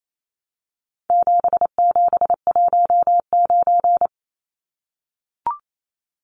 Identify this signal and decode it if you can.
Morse code, heard as a keyed tone, about 28 wpm, 700 Hz. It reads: 7719
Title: Numbers - 4-Digit 28wpm